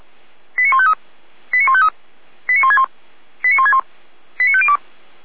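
Five-tone selective call (ZVEI), heard as quick runs of five tones, tones of about 70 ms. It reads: 89153 89153 89161 89161 89502
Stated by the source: ZVEI five-tone selective call received on SDR.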